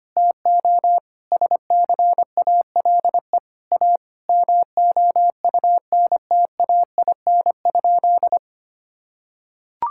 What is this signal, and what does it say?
Morse code, heard as a keyed tone, about 25 wpm, 700 Hz. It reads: TO SCALE A MOUNTAIN?
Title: Sentences from Top 1000 Words - Review Part 5 - 25wpm